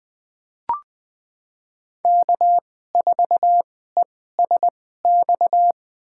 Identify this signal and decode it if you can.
Morse code, heard as a keyed tone, about 21 wpm, spaced as for 15 wpm, 700 Hz. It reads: K4ESX
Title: USA Call Signs - 5 Characters - 2x Character Spacing - 20wpm